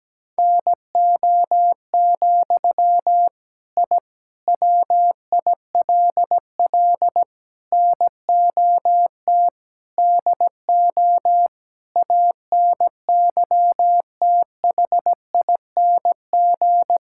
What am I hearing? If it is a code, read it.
Morse code, heard as a keyed tone, about 17 wpm, 700 Hz. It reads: NO, I WILL NOT DO ANYTHING
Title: Sentences from Top 1000 Words - Review Part 6 - 17wpm